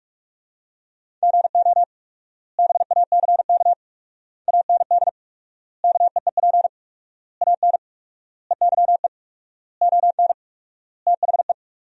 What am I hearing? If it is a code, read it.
Morse code, heard as a keyed tone, about 45 wpm, 700 Hz. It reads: GO BACK AND KEEP AN EYE ON THE
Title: 4x Word-Spacing - Sentences from Top 300 Words - 45wpm